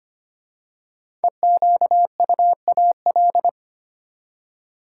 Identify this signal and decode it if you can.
Morse code, heard as a keyed tone, about 25 wpm, 700 Hz. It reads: EQUAL
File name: 5 Letter Words - First 3 Letters Given - 25wpm